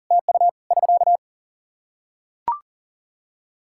Morse code, heard as a keyed tone, about 40 wpm, 700 Hz. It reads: TU <SK>
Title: Practice QSO - Exchange 4 - Ending - 40wpm